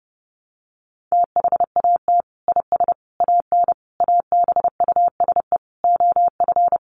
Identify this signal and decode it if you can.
Morse code, heard as a keyed tone, about 30 wpm, 700 Hz. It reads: THAT IS AN ABUSE OF
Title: Sentences from Top 1300 Words 30wpm